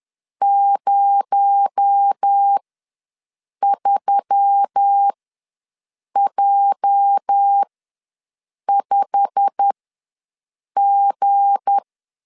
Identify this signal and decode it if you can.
Morse code, heard as a keyed tone, about 10 wpm, spaced as for 6 wpm, 800 Hz. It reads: 03J5G
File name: LEZIONI / 8 YJ45 / 8 YJ45 ws 10wpm 3843